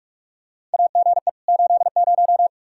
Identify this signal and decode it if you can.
Morse code, heard as a keyed tone, about 41 wpm, 700 Hz. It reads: AGE 80
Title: K1USN SST QSO Practice 45wpm